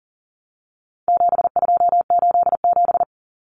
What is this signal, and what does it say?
Morse code, heard as a keyed tone, about 40 wpm, 700 Hz. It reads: 7287
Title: Common Cut Numbers - 4-Digit 40wpm